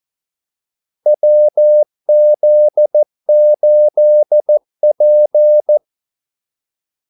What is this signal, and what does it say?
Morse code, heard as a keyed tone, about 14 wpm, 600 Hz. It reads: WZ8P
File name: Callsigns-14wpm